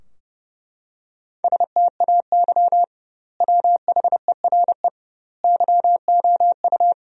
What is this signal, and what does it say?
Morse code, heard as a keyed tone, about 32 wpm, 700 Hz. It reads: STAY WHERE YOU
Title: 2x Word-Spacing - Sentences from Top 500 Words - 30wpm